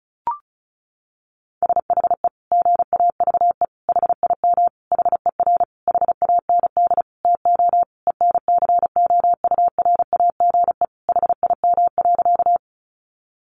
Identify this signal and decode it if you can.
Morse code, heard as a keyed tone, about 34 wpm, 700 Hz. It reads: SHE GAVE HIM HER HAND TO ENCOURAGE HIM.
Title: Sentences from Top 1400 Words 35wpm